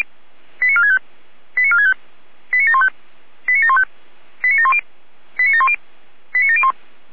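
Five-tone selective call (ZVEI), heard as quick runs of five tones, tones of about 70 ms. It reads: 89465 89465 89715 89715 89710 89710 89701
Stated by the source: ZVEI five-tone selective call 89531, received by SDR.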